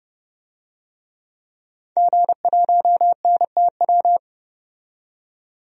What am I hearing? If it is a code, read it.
Morse code, heard as a keyed tone, about 31 wpm, 690 Hz. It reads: G1NTW
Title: DXCC-100 Call Signs - 5 Characters - 30wpm